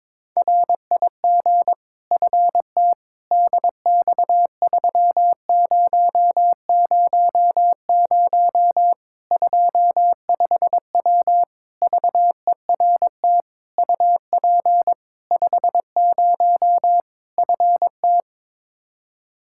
Morse code, heard as a keyed tone, about 22 wpm, 700 Hz. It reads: RIG FT DX3000 25W VERT UP 50 FT
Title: Practice QSO - Exchange 2 - Info - 22wpm